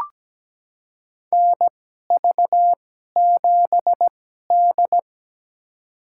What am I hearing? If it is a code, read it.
Morse code, heard as a keyed tone, about 17 wpm, spaced as for 12 wpm, 700 Hz. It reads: NV7D
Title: USA Call Signs - 4 Characters - 2x Character Spacing - 17wpm